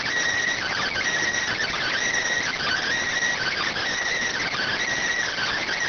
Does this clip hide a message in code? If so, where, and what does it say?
Yes: Emergency Alert System (SAME) tones (header bursts) from 0.5 to 4.8 s, heard as harsh buzzing bursts.